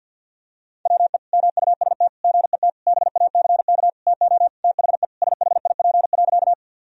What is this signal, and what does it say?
Morse code, heard as a keyed tone, about 51 wpm, 700 Hz. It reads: WE MUST GET BACK TO THE SHIP.
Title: Sentences from Top 1000 Words - Review Part 2 - 50wpm